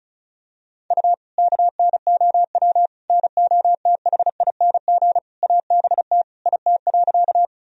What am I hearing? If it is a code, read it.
Morse code, heard as a keyed tone, about 36 wpm, 700 Hz. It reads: U KNOW NOTHING ABT IT.
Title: Sentences from Top 500 Words and Abbreviations 35wpm